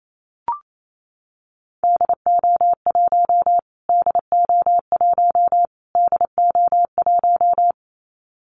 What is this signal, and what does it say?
Morse code, heard as a keyed tone, about 28 wpm, 700 Hz. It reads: DO1 DO1 DO1
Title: Callsign Prefixes for Top 25 DXCC - Repeated 3x - 28wpm